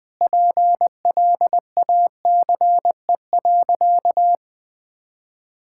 Morse code, heard as a keyed tone, about 20 wpm, 700 Hz. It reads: PLACE.
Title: Sentences with 8-Word Phrases - Spoken Beginning - 20wpm